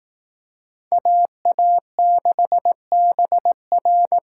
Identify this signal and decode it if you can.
Morse code, heard as a keyed tone, about 18 wpm, 700 Hz. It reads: AA6BR
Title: Speed Racing - USA 2x2 Call Signs 15wpm